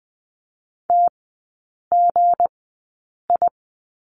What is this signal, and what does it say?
Morse code, heard as a keyed tone, about 20 wpm, spaced as for 8 wpm, 690 Hz. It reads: TGI
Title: 159 Lesson 26e - Letter g - Three Characters 20wpm